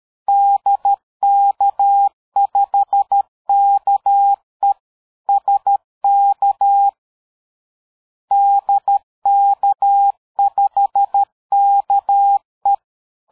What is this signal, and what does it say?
Morse code, heard as a keyed tone, about 13 wpm, 800 Hz. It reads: DK5KE SK DK5KE
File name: dk5ke / QSO 60BpM